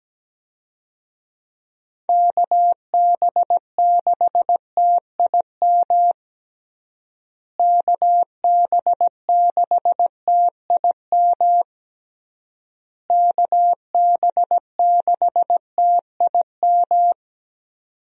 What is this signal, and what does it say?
Morse code, heard as a keyed tone, about 17 wpm, 700 Hz. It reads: KB6TIM KB6TIM KB6TIM
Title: USA Call Signs - 6 Characters - Repeated 3x - 17wpm